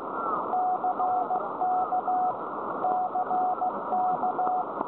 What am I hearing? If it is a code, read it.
Morse code, heard as a keyed tone, about 15 wpm, 740 Hz, with noise. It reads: CK CK
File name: storage / websdr G8HXT Wheathampstead 1347km 14149.0kHz